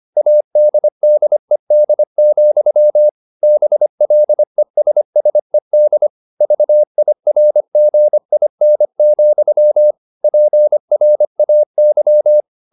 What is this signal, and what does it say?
Morse code, heard as a keyed tone, about 25 wpm, 600 Hz. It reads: ADDED, BLESSED VIRGIN, PRAY
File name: the old man and the sea part 1 25wpm